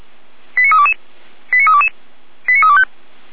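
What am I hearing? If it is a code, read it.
Five-tone selective call (ZVEI), heard as quick runs of five tones, tones of about 80 ms. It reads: 89322 89322 89325
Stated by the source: ZVEI five-tone call received by SDR, logged by Feuerwehr Orlamünde.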